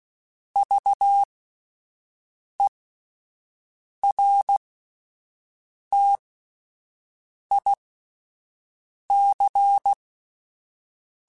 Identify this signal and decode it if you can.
Morse code, heard as a keyed tone, about 16 wpm, spaced as for 5 wpm, 800 Hz. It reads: VERTIC
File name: (spaced) / QSOs Spaced-mp3s / WA7QOX 2 @10 sp